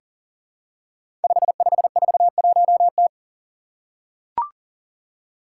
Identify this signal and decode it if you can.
Morse code, heard as a keyed tone, about 40 wpm, 700 Hz. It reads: 5541T
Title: Five Challenging Letter-Number Combinations 40wpm